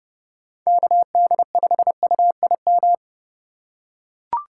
Morse code, heard as a keyed tone, about 30 wpm, 700 Hz. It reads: KD5UIM
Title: USA Call Signs - 2x Character Spacing - 30wpm